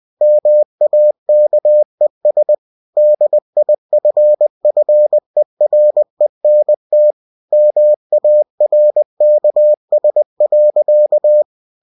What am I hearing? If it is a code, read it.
Morse code, heard as a keyed tone, about 20 wpm, 600 Hz. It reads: MAKES DIFFERENT MARKS.
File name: cw-books / winnie-the-pooh / winnie-20wpm-page-0023